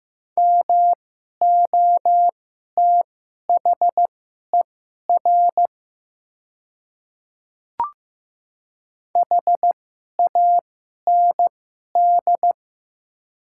Morse code, heard as a keyed tone, about 15 wpm, spaced as for 11 wpm, 700 Hz. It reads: MOTHER HAND
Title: 2x Character Spacing - Top 200 Words - 15wpm